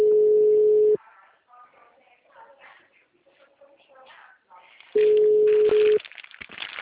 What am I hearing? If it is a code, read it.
Ringback tone (Europe, 425 Hz), with speech.